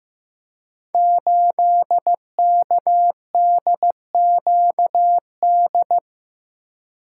Morse code, heard as a keyed tone, about 15 wpm, 700 Hz. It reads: 8KDQD
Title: Code Groups - 5 Letter:Numbers - 15wpm